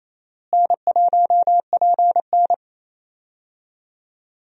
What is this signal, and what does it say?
Morse code, heard as a keyed tone, about 27 wpm, 700 Hz. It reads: N1PN
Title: USA Call Signs - 4 Characters - Repeated 2x - 28wpm